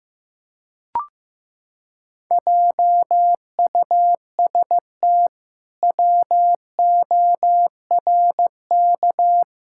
Morse code, heard as a keyed tone, about 15 wpm, 700 Hz. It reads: JUST WORK
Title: Pangram Challenge v2 - 15wpm